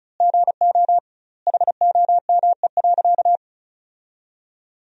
Morse code, heard as a keyed tone, about 34 wpm, 700 Hz. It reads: GO HOME.
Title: 2x Word-Spacing - Sentences from Top 300 Words - 35wpm